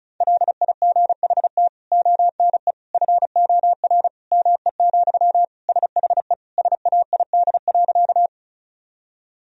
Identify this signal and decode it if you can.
Morse code, heard as a keyed tone, about 34 wpm, 700 Hz. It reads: LIGHT ONE FOR ME, SHE SAID.
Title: Sentences from Top 1000 Words - Review Part 1 - 35wpm